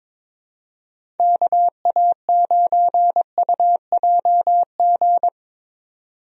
Morse code, heard as a keyed tone, about 22 wpm, 700 Hz. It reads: KA9UJG
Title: USA Call Signs - 6 Characters - 22wpm